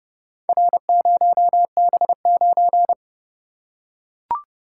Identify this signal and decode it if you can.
Morse code, heard as a keyed tone, about 30 wpm, 700 Hz. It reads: R0B9